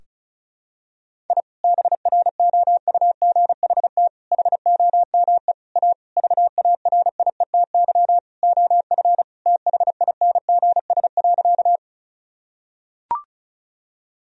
Morse code, heard as a keyed tone, about 36 wpm, 700 Hz. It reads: I BROUGHT HOME A VARIETY OF THINGS.
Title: Sentences from Top 1000 Words - Review Part 4 - 35wpm